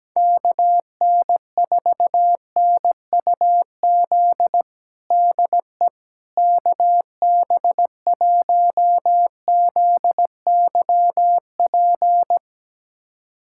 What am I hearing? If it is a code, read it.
Morse code, heard as a keyed tone, about 17 wpm, 700 Hz. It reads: KN4NUZ DE KB1ZYP
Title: USA Callsign DE Callsign 17wpm